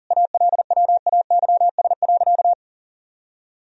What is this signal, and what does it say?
Morse code, heard as a keyed tone, about 37 wpm, 700 Hz. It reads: ALWAYS.
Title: Sentences from Top 1000 Words - Review Part 4 - 40wpm